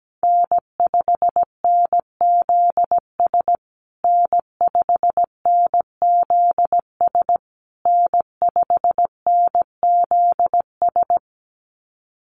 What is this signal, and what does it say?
Morse code, heard as a keyed tone, about 17 wpm, 700 Hz. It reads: N5NZS N5NZS N5NZS
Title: USA Call Signs Repeated 3x 17wpm